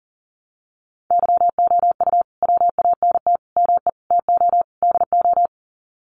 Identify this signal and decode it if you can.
Morse code, heard as a keyed tone, about 40 wpm, 700 Hz. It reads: YOU WANT ME TO DO
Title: Sets of 5 Words Encore2 40wpm